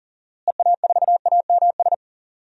Morse code, heard as a keyed tone, about 40 wpm, 700 Hz. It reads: EA4AMS
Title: DXCC-100 Call Signs - 6 Characters - Repeated 3x - 40wpm